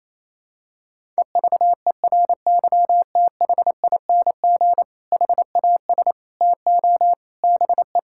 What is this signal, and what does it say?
Morse code, heard as a keyed tone, about 28 wpm, 700 Hz. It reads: EVERYTHING HAS TO BE
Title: Sentences from Top 1600 Words 28wpm